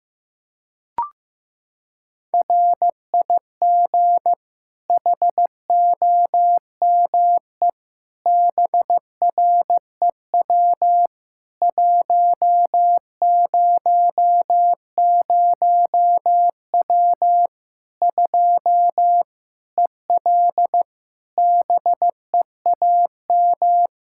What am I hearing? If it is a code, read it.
Morse code, heard as a keyed tone, about 15 wpm, 700 Hz. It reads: RIG HOME BREW 100W 2 EL BEAM
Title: Practice QSO - Exchange 2 - Info - 15wpm